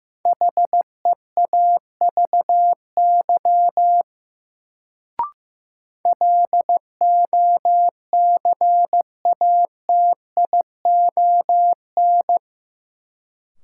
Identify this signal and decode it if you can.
Morse code, heard as a keyed tone, about 15 wpm, 700 Hz. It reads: HEAVY LOCATION
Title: Sentences from Top 1000 Words - Review Part 6 - New Word Drill - 15wpm